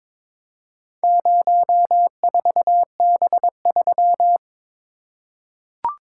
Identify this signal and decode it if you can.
Morse code, heard as a keyed tone, about 22 wpm, 700 Hz. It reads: 04B3
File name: practice / 22wpm / Four Letter-Number Combinations - 22wpm